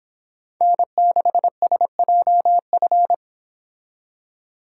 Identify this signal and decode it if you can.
Morse code, heard as a keyed tone, about 28 wpm, 700 Hz. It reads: N6SJF